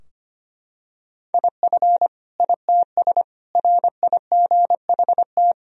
Morse code, heard as a keyed tone, about 25 wpm, 700 Hz. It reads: IF ITS RIGHT
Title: Sentences from Top 1000 Words - Review Part 3 - 25wpm